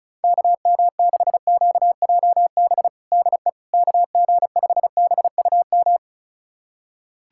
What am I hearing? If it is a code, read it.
Morse code, heard as a keyed tone, about 34 wpm, 700 Hz. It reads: KM6QJB DE KG5BUM
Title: USA Callsign DE Callsign - 6 Characters - 35wpm